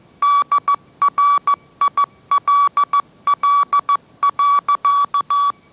Morse code, heard as a keyed tone, about 14 wpm, 1150 Hz. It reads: DRILL.
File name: mrx / mrx065 16wpm